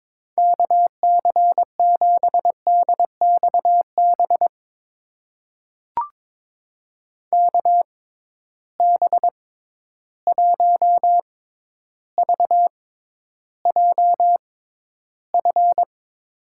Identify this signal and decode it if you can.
Morse code, heard as a keyed tone, about 22 wpm, spaced as for 11 wpm, 700 Hz. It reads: KC7DXB KB1VJF
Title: USA Call Signs - 6 Characters - 6x Character Spacing - 22wpm